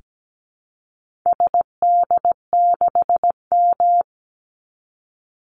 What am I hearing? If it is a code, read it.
Morse code, heard as a keyed tone, about 17 wpm, 700 Hz. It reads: SD6M
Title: Call Signs - 17wpm